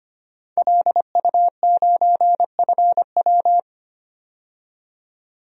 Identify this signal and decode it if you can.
Morse code, heard as a keyed tone, about 25 wpm, 700 Hz. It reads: LU9FW
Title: DXCC-100 Call Signs - 5 Characters - 25wpm